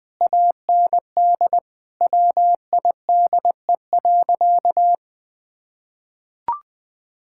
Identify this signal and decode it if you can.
Morse code, heard as a keyed tone, about 21 wpm, 700 Hz. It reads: AND WIDE.
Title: Sentences from Top 1000 Words - Review Part 6 - 20wpm